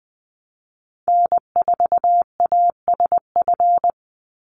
Morse code, heard as a keyed tone, about 20 wpm, 700 Hz. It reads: N4ASF
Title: Call Signs Repeated 3x - 20wpm